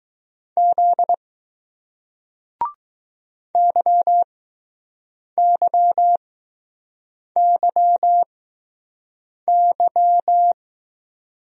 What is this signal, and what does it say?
Morse code, heard as a keyed tone, about 20 wpm, spaced as for 6 wpm, 700 Hz. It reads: Z YYYY